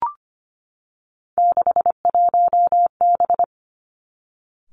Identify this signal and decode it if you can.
Morse code, heard as a keyed tone, about 25 wpm, 700 Hz. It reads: 61B
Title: Three Challenging Letter-Number Combinations 25wpm